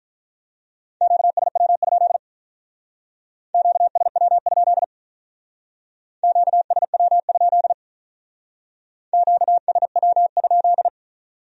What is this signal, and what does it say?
Morse code, heard as a keyed tone, about 44 wpm, 700 Hz. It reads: QSW? QSW? QSW? QSW?